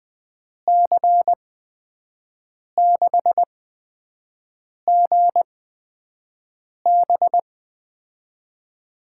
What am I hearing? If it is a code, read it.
Morse code, heard as a keyed tone, about 20 wpm, spaced as for 5 wpm, 700 Hz. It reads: C6GB
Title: USA Call Signs - 5 Characters - 8x Character Spacing - 20wpm